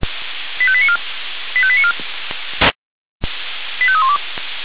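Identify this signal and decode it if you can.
Five-tone selective call (ZVEI), heard as quick runs of five tones, tones of about 70 ms. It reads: 85894 85894 85312
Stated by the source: Received by SDR as a ZVEI five-tone selective call.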